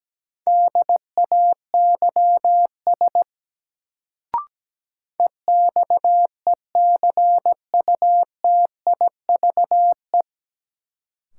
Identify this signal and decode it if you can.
Morse code, heard as a keyed tone, about 17 wpm, 700 Hz. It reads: DAYS EXECUTIVE
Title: Sets of 5 Words Encore - New Word Drill - 17wpm